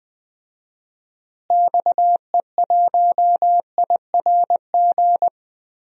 Morse code, heard as a keyed tone, about 20 wpm, 700 Hz. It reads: XE1IRG
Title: DXCC-100 Call Signs - 6 Characters - Repeated 2x - 20wpm